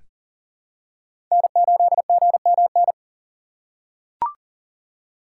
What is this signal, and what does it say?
Morse code, heard as a keyed tone, about 40 wpm, 700 Hz. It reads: N8GMN
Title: USA Call Signs - 5 Characters - 3x Character Spacing - 40wpm